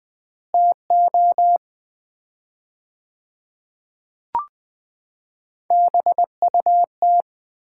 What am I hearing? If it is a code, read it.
Morse code, heard as a keyed tone, about 20 wpm, 700 Hz. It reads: TO BUT
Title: 4x Word-Spacing - Sets of 3 Words - 20wpm